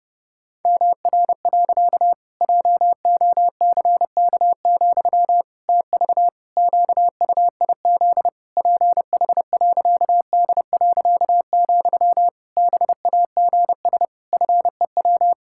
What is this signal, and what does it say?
Morse code, heard as a keyed tone, about 30 wpm, 700 Hz. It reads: MR. JOCK, TV QUIZ PH.D., BAGS FEW
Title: Pangram Challenge v2 - 30wpm